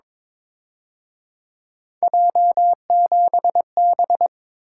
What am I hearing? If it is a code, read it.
Morse code, heard as a keyed tone, about 26 wpm, 700 Hz. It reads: J7B
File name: Speed-Racing - Three Letter-Number Combinations - 22wpm